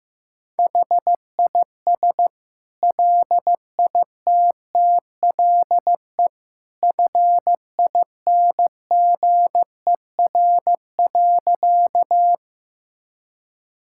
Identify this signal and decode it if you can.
Morse code, heard as a keyed tone, about 15 wpm, 700 Hz. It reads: HIS LITTLE FINGER.
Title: Sentences from Top 1600 Words 15wpm